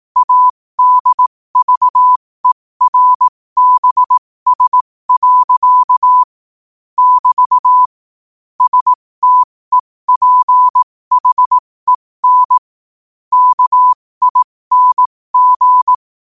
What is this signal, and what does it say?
Morse code, heard as a keyed tone, about 18 wpm, spaced as for 15 wpm, 1000 Hz. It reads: ADVERBS. <BT> STEPHEN KING
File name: qotdfiles / 15wpm